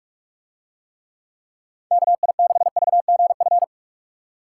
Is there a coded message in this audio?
Yes: Morse code from 1.9 to 3.6 s, heard as a keyed tone, about 45 wpm, 700 Hz. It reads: KI6VGF